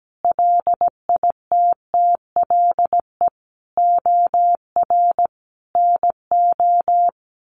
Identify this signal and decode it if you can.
Morse code, heard as a keyed tone, about 17 wpm, 700 Hz. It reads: LITTLE OR NO
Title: Sentences from Top 1400 Words 17wpm